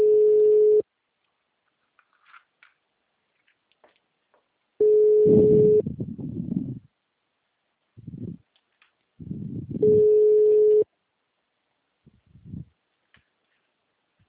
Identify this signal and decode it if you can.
Ringback tone (Europe, 425 Hz).